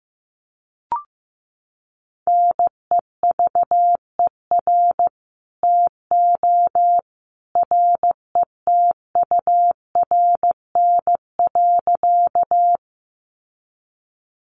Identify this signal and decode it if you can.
Morse code, heard as a keyed tone, about 15 wpm, 700 Hz. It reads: NEVER TO RETURN.
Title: Sentences from Top 800 Words 15wpm